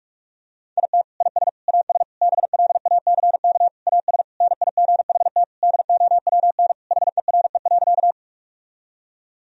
Morse code, heard as a keyed tone, about 46 wpm, 700 Hz. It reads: IT IS AS BLACK AS NIGHT DOWN HERE.
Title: Sentences from Top 1000 Words - Review Part 2 - 45wpm